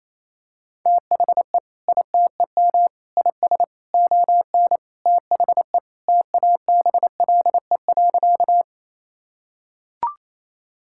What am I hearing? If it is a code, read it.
Morse code, heard as a keyed tone, about 28 wpm, 700 Hz. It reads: THE ITEM IS ON THE TABLE.
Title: Sentences from Top 1000 Words - Review Part 6 - 28wpm